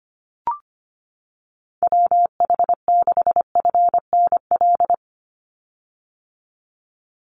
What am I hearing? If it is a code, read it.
Morse code, heard as a keyed tone, about 25 wpm, 700 Hz. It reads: WH6FNL
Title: USA Call Signs 25wpm